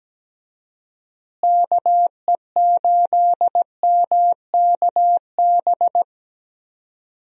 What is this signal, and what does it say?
Morse code, heard as a keyed tone, about 17 wpm, 700 Hz. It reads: KE8MKB
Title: USA Call Signs - 6 Characters - Repeated 2x - 17wpm